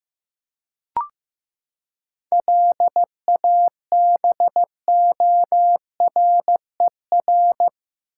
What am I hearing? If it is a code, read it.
Morse code, heard as a keyed tone, about 16 wpm, 700 Hz. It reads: LABORER
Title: K1USN SST QSO Practice 15wpm